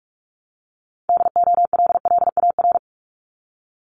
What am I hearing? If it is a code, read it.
Morse code, heard as a keyed tone, about 46 wpm, 700 Hz. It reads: DOLLAR
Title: Multi-letter Phonemes 45wpm